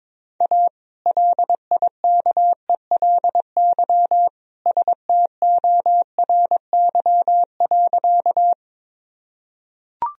Morse code, heard as a keyed tone, about 22 wpm, 700 Hz. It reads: A LIKELY STORY.
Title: Sentences from Top 1000 Words - Review Part 4 - 22wpm